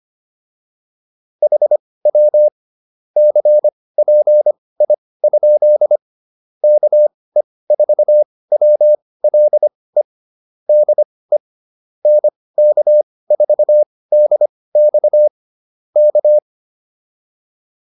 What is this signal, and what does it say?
Morse code, heard as a keyed tone, about 25 wpm, spaced as for 18 wpm, 600 Hz. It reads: HW CPI? KE4WLE DE NK4DX K
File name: QSO110 18